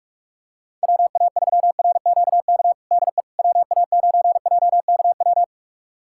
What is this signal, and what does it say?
Morse code, heard as a keyed tone, about 41 wpm, 700 Hz. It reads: WA3RQK DE WA9JKW
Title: USA Callsign DE Callsign - 6 Characters - 45wpm